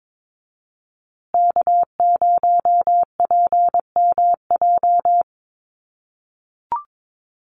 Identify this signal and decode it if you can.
Morse code, heard as a keyed tone, about 22 wpm, 700 Hz. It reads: K0PMJ
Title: USA Call Signs Repeated 3x 22wpm